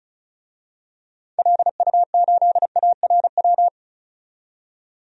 Morse code, heard as a keyed tone, about 36 wpm, 700 Hz. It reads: LU8ARW
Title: DXCC-100 Call Signs - 6 Characters - 35wpm